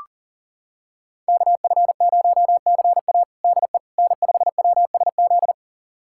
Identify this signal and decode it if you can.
Morse code, heard as a keyed tone, about 40 wpm, 700 Hz. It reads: KF0CA DE N5WSZ
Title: USA Callsign DE Callsign - 5 Characters - 40wpm